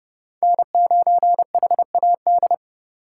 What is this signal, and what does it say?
Morse code, heard as a keyed tone, about 36 wpm, 700 Hz. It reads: N9HAD
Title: Speed Racing - USA 1x3 Call Signs 30wpm